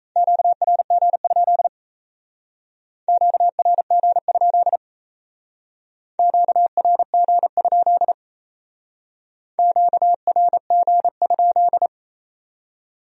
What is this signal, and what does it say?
Morse code, heard as a keyed tone, about 35 wpm, 700 Hz. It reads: QRG? QRG? QRG? QRG?